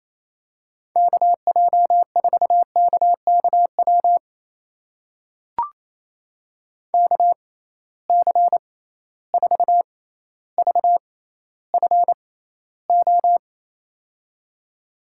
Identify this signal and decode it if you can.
Morse code, heard as a keyed tone, about 27 wpm, spaced as for 14 wpm, 700 Hz. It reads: KJ4KKW KC4VFO
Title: USA Call Signs - 6 Characters - 6x Character Spacing - 28wpm